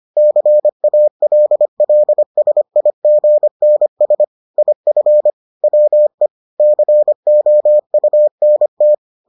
Morse code, heard as a keyed tone, about 25 wpm, 600 Hz. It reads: CALLSIGNS IF WE COUNT